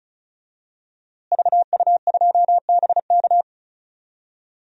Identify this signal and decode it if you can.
Morse code, heard as a keyed tone, about 34 wpm, 700 Hz. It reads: VU2BK
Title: DXCC-100 Call Signs - 5 Characters - Repeated 3x - 35wpm